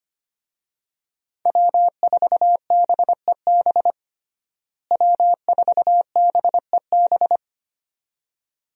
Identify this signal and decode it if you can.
Morse code, heard as a keyed tone, about 25 wpm, 700 Hz. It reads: W4BEB W4BEB
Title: USA Call Signs - 5 Characters - Repeated 2x - 25wpm